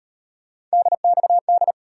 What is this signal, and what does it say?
Morse code, heard as a keyed tone, about 37 wpm, 700 Hz. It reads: DXD